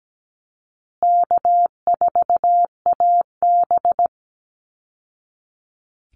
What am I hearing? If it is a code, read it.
Morse code, heard as a keyed tone, about 17 wpm, 700 Hz. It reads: K4AB
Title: Call Signs Repeated 2x - 17wpm